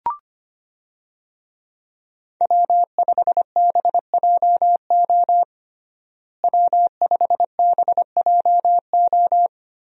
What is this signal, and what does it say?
Morse code, heard as a keyed tone, about 25 wpm, 700 Hz. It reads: W5BJO W5BJO
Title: USA Call Signs - 5 Characters - Repeated 3x - 25wpm